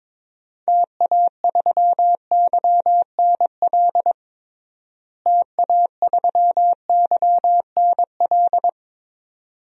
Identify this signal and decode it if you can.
Morse code, heard as a keyed tone, about 22 wpm, 700 Hz. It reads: TA3YNL TA3YNL
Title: DXCC-100 Call Signs - 6 Characters - Repeated 2x - 22wpm